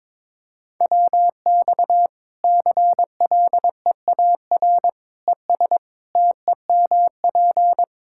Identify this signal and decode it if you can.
Morse code, heard as a keyed tone, about 22 wpm, 700 Hz. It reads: WX CLEAR ES TEMP
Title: Practice QSO - Exchange 2 - Info - 22wpm